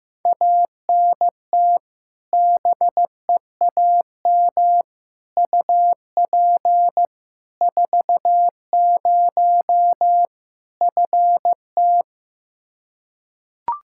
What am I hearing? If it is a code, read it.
Morse code, heard as a keyed tone, about 15 wpm, 700 Hz. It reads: ANT BEAM UP 40 FT
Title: Practice QSO - Exchange 2 - Info - 15wpm